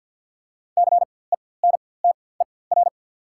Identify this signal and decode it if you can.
Morse code, heard as a keyed tone, about 48 wpm, spaced as for 23 wpm, 690 Hz. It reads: CENTER
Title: 4x Character Spacing - Top 500 Words - 50wpm